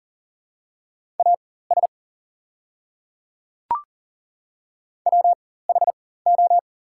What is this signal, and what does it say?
Morse code, heard as a keyed tone, about 40 wpm, spaced as for 19 wpm, 690 Hz. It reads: AS WHO
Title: 4x Character Spacing - Top 100 Words - 40wpm